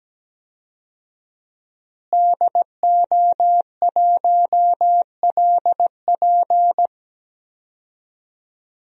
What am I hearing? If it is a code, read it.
Morse code, heard as a keyed tone, about 17 wpm, 700 Hz. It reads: DO1LP
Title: DXCC-100 Call Signs - 5 Characters - Repeated 2x - 17wpm